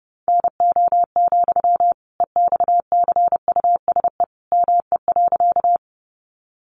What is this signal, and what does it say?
Morse code, heard as a keyed tone, about 30 wpm, 700 Hz. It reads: NO, EXCUSE ME.
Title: Sentences from Top 700 Words 30wpm